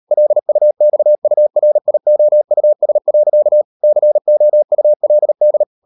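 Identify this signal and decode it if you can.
Morse code, heard as a keyed tone, about 38 wpm, 600 Hz. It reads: LUXURIOUS. COULD
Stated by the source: Morse code practice at 38 WPM, sending Ken Follett's novel Never in English, from CW med Gnister.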